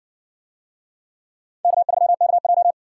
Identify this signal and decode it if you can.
Morse code, heard as a keyed tone, about 58 wpm, 700 Hz. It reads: K3KJ